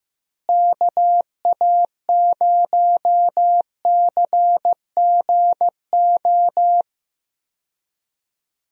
Morse code, heard as a keyed tone, about 16 wpm, 700 Hz. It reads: KA0CGO
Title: USA Call Signs - 6 Characters - Repeated 3x - 15wpm